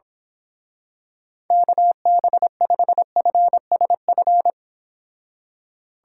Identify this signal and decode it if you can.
Morse code, heard as a keyed tone, about 26 wpm, 700 Hz. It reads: KB5FSF
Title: Speed Racing - USA 2x3 Call Signs 22wpm